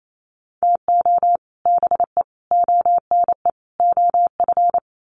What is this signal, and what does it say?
Morse code, heard as a keyed tone, about 28 wpm, 700 Hz. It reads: TO BE ONE OF
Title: Sets of 4 Words 28wpm